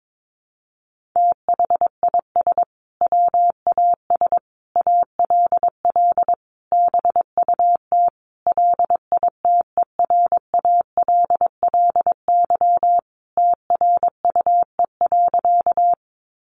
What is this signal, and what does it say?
Morse code, heard as a keyed tone, about 22 wpm, 700 Hz. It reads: THIS WAS ALL BUT LITERALLY TRUE.
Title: Sentences from Top 1400 Words 22wpm